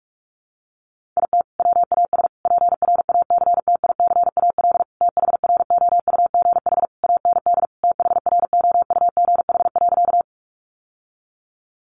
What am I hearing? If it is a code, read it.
Morse code, heard as a keyed tone, about 44 wpm, 700 Hz. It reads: IT WAS PRACTICAL THROUGH AND THROUGH.
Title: Sentences from Top 1400 Words 45wpm